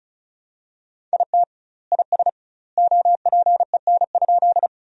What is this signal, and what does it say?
Morse code, heard as a keyed tone, about 34 wpm, 700 Hz. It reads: IT IS OPEN?
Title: 2x Word-Spacing - Sentences from Top 300 Words - 35wpm